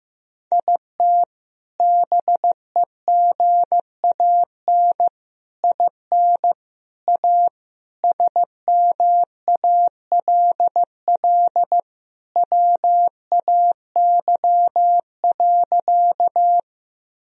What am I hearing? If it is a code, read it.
Morse code, heard as a keyed tone, about 15 wpm, 700 Hz. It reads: IT BEGAN IN A SMALL WAY.
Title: Sentences from Top 1000 Words - Review Part 1 - 15wpm